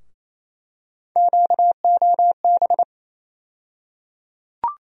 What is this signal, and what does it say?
Morse code, heard as a keyed tone, about 28 wpm, 700 Hz. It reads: QOB
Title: USA Call Sign Postfixes - 3 Letters 28wpm